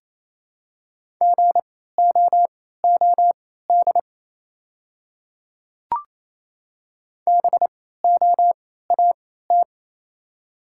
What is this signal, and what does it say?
Morse code, heard as a keyed tone, about 28 wpm, spaced as for 16 wpm, 700 Hz. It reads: GOOD BOAT
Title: 3x Character Spacing - Top 500 Words - 28wpm